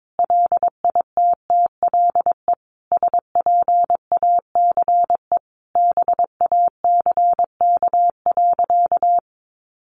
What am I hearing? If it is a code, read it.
Morse code, heard as a keyed tone, about 22 wpm, 700 Hz. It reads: LITTLE SPACE BACK.
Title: Sentences from Top 1500 Words 22wpm